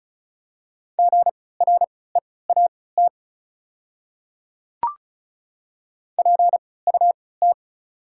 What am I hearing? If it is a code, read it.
Morse code, heard as a keyed tone, about 36 wpm, spaced as for 20 wpm, 700 Hz. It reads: GREAT PUT
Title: 3x Character Spacing - Top 200 Words - 35wpm